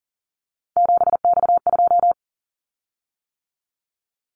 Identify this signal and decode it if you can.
Morse code, heard as a keyed tone, about 39 wpm, 700 Hz. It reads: 7X2
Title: Callsign Prefixes for Top 100 DXCC - 40wpm